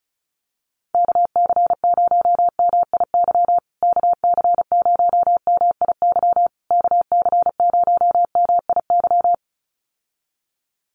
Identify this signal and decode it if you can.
Morse code, heard as a keyed tone, about 34 wpm, 700 Hz. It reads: KC0MIY KC0MIY KC0MIY
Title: USA Call Signs Repeated 3x 35wpm